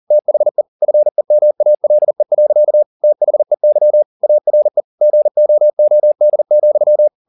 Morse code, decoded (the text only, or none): THE FEMALE. THEY ARE GOOD,